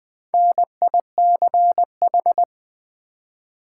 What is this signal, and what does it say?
Morse code, heard as a keyed tone, about 20 wpm, 700 Hz. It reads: NICH
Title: Build A Word - 1 Letter At A Time - 3 to 5-Letter Words - 20wpm